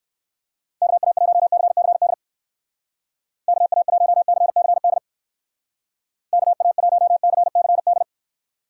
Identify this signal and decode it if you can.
Morse code, heard as a keyed tone, about 57 wpm, 700 Hz. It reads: KA1CCD KA1CCD KA1CCD